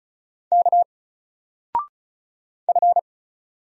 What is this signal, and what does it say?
Morse code, heard as a keyed tone, about 34 wpm, 700 Hz. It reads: K F
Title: Single Letter-Number 35wpm